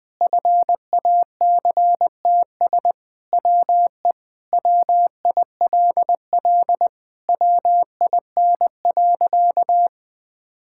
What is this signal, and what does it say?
Morse code, heard as a keyed tone, about 21 wpm, 700 Hz. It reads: FACTS WE WILL WIN.
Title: Sentences with 3-Word Phrases - Spoken Beginning - 20wpm